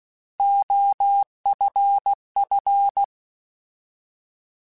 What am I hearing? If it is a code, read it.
Morse code, heard as a keyed tone, about 16 wpm, 800 Hz. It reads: OFF
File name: Alphabetical / Alphabetical mp3s / Os @15